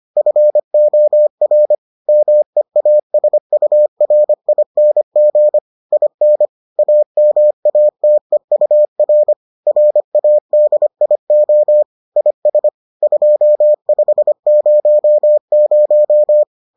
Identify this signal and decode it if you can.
Morse code, heard as a keyed tone, about 25 wpm, 600 Hz. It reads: FOR MEASURING IN AMATEUR RADIO IS 2500